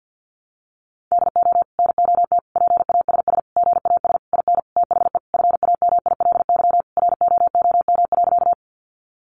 Morse code, heard as a keyed tone, about 50 wpm, 700 Hz. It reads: DO NOT PASS GAS IN THE FAMILY ROOM.
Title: Sentences from Top 1100 Words 50wpm